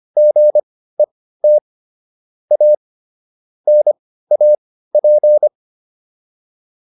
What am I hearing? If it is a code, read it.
Morse code, heard as a keyed tone, about 25 wpm, spaced as for 15 wpm, 600 Hz. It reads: GET A NAP